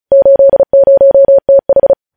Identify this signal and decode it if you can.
Morse code, heard as a keyed tone, about 36 wpm, 550 Hz. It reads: 80TH
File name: podcast / News headlines-1x-35x35